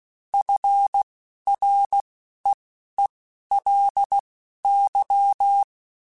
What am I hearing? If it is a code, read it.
Morse code, heard as a keyed tone, about 16 wpm, spaced as for 11 wpm, 800 Hz. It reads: FREELY